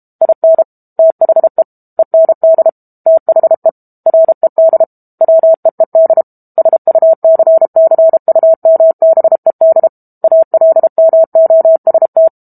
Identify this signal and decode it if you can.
Morse code, heard as a keyed tone, about 32 wpm, 650 Hz. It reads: IN THE END THE RED WEED SUCCUMBED ALMOST
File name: Book Two / 06-Chapter 6 - 32 WPM